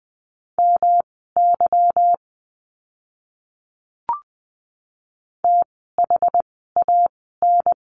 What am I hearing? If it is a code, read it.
Morse code, heard as a keyed tone, about 20 wpm, spaced as for 15 wpm, 700 Hz. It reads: MY THAN
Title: 185 Lesson 30f - Letter b - Words 20wpm